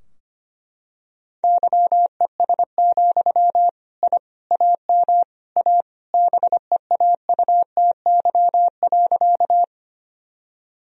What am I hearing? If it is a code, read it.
Morse code, heard as a keyed tone, about 25 wpm, 700 Hz. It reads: YES, I AM A BEAUTY.
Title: Sentences from Top 1000 Words - Review Part 2 - 25wpm